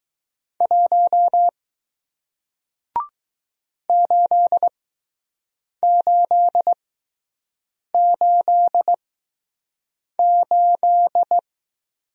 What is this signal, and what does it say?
Morse code, heard as a keyed tone, about 20 wpm, spaced as for 6 wpm, 700 Hz. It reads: 1 8888